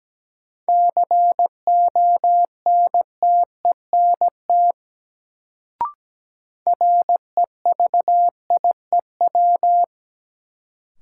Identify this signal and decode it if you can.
Morse code, heard as a keyed tone, about 17 wpm, 700 Hz. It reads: CONTENT REVIEW
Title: Sentences from Top 900 Words - New Word Drill - 17wpm